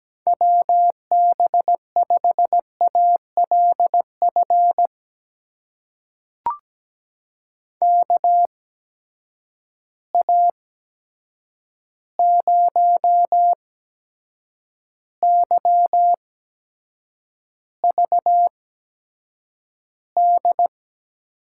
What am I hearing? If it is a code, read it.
Morse code, heard as a keyed tone, about 17 wpm, spaced as for 7 wpm, 700 Hz. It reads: WB5ALF KA0YVD
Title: USA Call Signs - 6 Characters - 8x Character Spacing - 17wpm